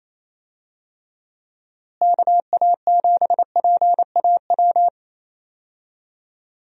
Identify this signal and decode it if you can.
Morse code, heard as a keyed tone, about 27 wpm, 700 Hz. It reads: KA7PAW
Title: USA Call Signs - 6 Characters - Repeated 3x - 28wpm